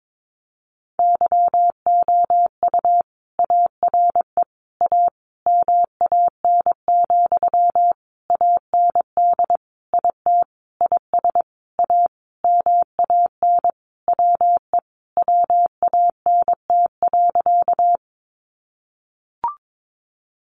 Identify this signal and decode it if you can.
Morse code, heard as a keyed tone, about 22 wpm, 700 Hz. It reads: YOU ARE A MAN, AND IT IS A MAN WE WANT.
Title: Sentences from Top 200 Words 22wpm